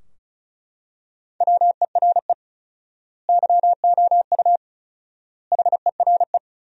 Morse code, heard as a keyed tone, about 34 wpm, 700 Hz. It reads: WERE YOU HERE
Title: 4x Word-Spacing - Sentences from Top 200 Words - 35wpm